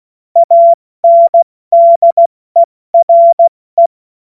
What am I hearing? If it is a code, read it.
Morse code, heard as a keyed tone, about 16 wpm, 680 Hz. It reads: ANDERE
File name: Witz-28-Oktober-2025-70-F80